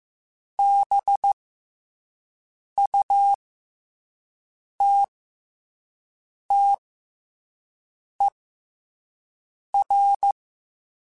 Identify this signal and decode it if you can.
Morse code, heard as a keyed tone, about 15 wpm, spaced as for 5 wpm, 800 Hz. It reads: BUTTER
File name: Special INGs-mp3s / Special INGs-mp3s / SlowToFaster ing @10